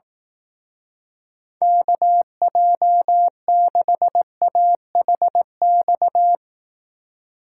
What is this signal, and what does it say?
Morse code, heard as a keyed tone, about 18 wpm, 700 Hz. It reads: KJ6AHX
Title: Speed Racing - USA 2x3 Call Signs 15wpm